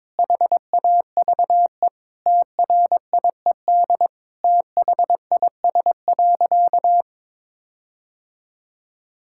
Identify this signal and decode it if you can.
Morse code, heard as a keyed tone, about 22 wpm, 700 Hz. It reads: HAVE TRIED THIS.
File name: practice / 22wpm / Sentences with 4-Word Phrases - 22wpm